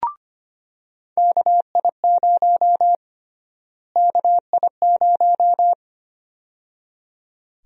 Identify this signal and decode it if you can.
Morse code, heard as a keyed tone, about 25 wpm, 700 Hz. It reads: KI0 KI0
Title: USA Call Sign Prefixes - Repeated 2x - 25wpm